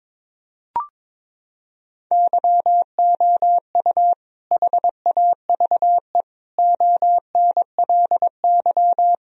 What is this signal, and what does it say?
Morse code, heard as a keyed tone, about 22 wpm, 700 Hz. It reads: YOU HAVE ONLY
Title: Sentences from Top 1000 Words - Review Part 2 - 22wpm